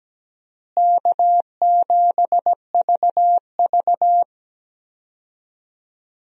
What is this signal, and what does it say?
Morse code, heard as a keyed tone, about 17 wpm, 700 Hz. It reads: K7VV
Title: USA Call Signs - 4 Characters - Repeated 3x - 17wpm